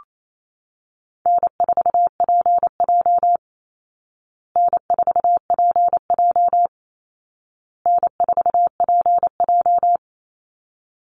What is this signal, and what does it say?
Morse code, heard as a keyed tone, about 28 wpm, 700 Hz. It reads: N4PJ N4PJ N4PJ
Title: Easy Call Signs Repeated 3x - 28wpm